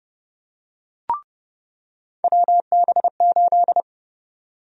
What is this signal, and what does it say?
Morse code, heard as a keyed tone, about 30 wpm, 700 Hz. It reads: WB8
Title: USA Call Sign Prefixes - 30wpm